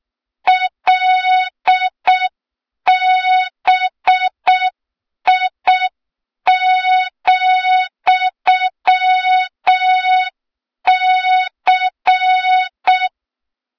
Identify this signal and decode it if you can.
Morse code, heard as a keyed tone, about 6 wpm, 3700 Hz, with noise. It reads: LBI,C
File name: Morze-HA2MN / hang / 23 GYAKORLAT